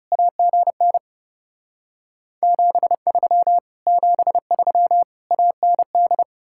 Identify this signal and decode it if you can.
Morse code, heard as a keyed tone, about 40 wpm, 700 Hz. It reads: AGN 73 73 AND